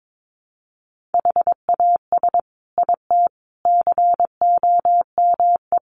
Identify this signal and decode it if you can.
Morse code, heard as a keyed tone, about 22 wpm, 700 Hz. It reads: HAS IT COME
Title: Sentences from Top 100 Words 22wpm